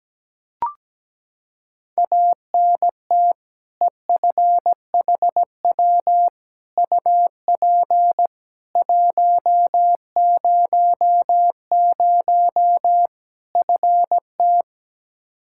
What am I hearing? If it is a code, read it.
Morse code, heard as a keyed tone, about 17 wpm, 700 Hz. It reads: ANT EFHW UP 100 FT
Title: Practice QSO - Exchange 2 - Info - 17wpm